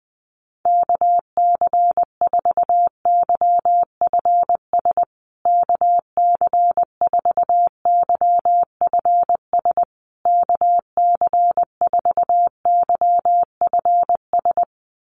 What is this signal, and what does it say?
Morse code, heard as a keyed tone, about 20 wpm, 700 Hz. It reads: KC4YFS KC4YFS KC4YFS
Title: USA Call Signs Repeated 3x 20wpm